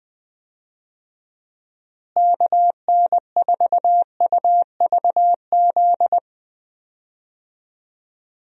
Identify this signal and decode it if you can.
Morse code, heard as a keyed tone, about 21 wpm, 700 Hz. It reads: KN4UVZ
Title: USA Call Signs - 6 Characters - Repeated 2x - 20wpm